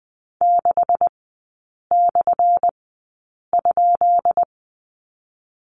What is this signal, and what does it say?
Morse code, heard as a keyed tone, about 20 wpm, spaced as for 8 wpm, 700 Hz. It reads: 6/?
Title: 178 Lesson 29e - Slash - Three Characters 20wpm